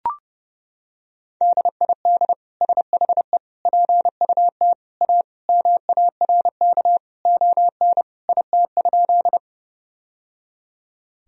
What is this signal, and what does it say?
Morse code, heard as a keyed tone, about 30 wpm, 700 Hz. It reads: DID SHE PUT A MARK ON IT?
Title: Sentences from Top 1000 Words - Review Part 1 - 30wpm